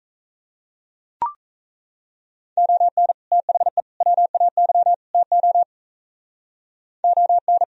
Morse code, heard as a keyed tone, about 34 wpm, 700 Hz. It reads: ON THE WAY TO ON
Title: Speed-Racing - Sets of 4 Words 28wpm